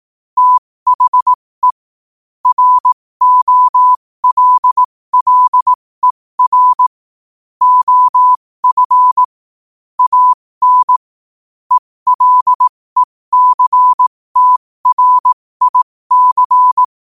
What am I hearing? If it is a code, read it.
Morse code, heard as a keyed tone, about 18 wpm, spaced as for 15 wpm, 1000 Hz. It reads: THE ROLLER OF AN ELECTRIC
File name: qotdfiles / 15wpm